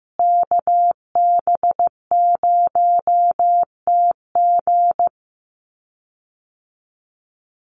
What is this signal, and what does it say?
Morse code, heard as a keyed tone, about 15 wpm, 700 Hz. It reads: KB0TG
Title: USA Call Signs Repeated 3x 15wpm